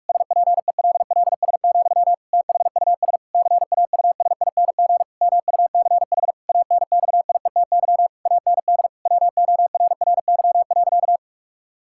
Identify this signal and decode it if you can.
Morse code, heard as a keyed tone, about 46 wpm, 700 Hz. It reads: SPELLS, THUS CAUSING MUCH ANXIETY AND WORRY.
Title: Pangram Challenge v2 - 45wpm